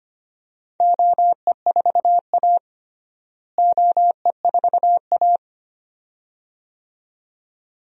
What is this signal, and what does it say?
Morse code, heard as a keyed tone, about 25 wpm, 700 Hz. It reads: OE4A OE4A
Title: DXCC-100 Call Signs - 4 Characters - Repeated 2x - 25wpm